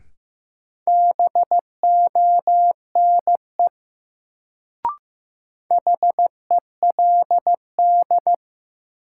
Morse code, heard as a keyed tone, about 15 wpm, 700 Hz. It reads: BONE HELD
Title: Sentences from Top 1600 Words - New Word Drill - 15wpm